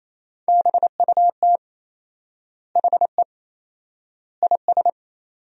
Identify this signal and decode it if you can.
Morse code, heard as a keyed tone, about 27 wpm, 700 Hz. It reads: BUT HE IS
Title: 4x Word-Spacing - Sentences from Top 200 Words - 28wpm